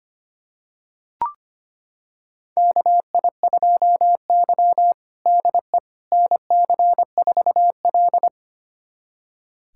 Morse code, heard as a keyed tone, about 25 wpm, 700 Hz. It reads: KI2Y DE NC4L
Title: USA Callsign DE Callsign - 4 Characters - 25wpm